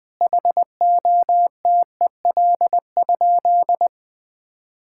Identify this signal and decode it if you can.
Morse code, heard as a keyed tone, about 21 wpm, 700 Hz. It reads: HOTEL?
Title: Sentences from Top 1600 Words 20wpm